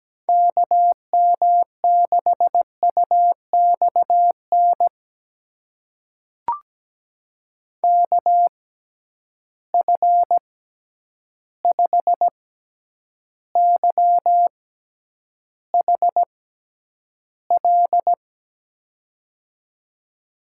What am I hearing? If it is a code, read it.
Morse code, heard as a keyed tone, about 17 wpm, spaced as for 9 wpm, 700 Hz. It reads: KM6UXN KF5YHL
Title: USA Call Signs - 6 Characters - 6x Character Spacing - 17wpm